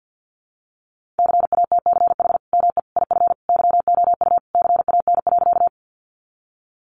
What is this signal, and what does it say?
Morse code, heard as a keyed tone, about 50 wpm, 700 Hz. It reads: CATCH ME IF YOU CAN.
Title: Sentences from Top 1500 Words 50wpm